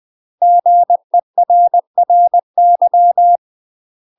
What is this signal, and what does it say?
Morse code, heard as a keyed tone, about 20 wpm, 700 Hz. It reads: GERRY
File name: UK MASTER QSOs TRAINING / 20WPM / UK MASTER QSOs 0040